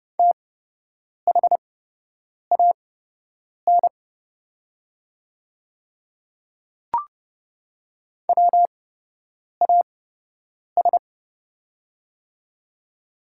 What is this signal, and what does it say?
Morse code, heard as a keyed tone, about 30 wpm, spaced as for 8 wpm, 700 Hz. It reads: THAN WAS
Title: 8x Character Spacing - Top 100 Words - 30wpm